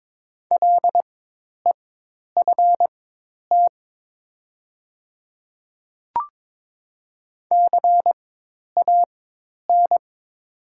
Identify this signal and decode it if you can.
Morse code, heard as a keyed tone, about 22 wpm, spaced as for 10 wpm, 700 Hz. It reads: LEFT CAN
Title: 4x Character Spacing - Top 500 Words - 22wpm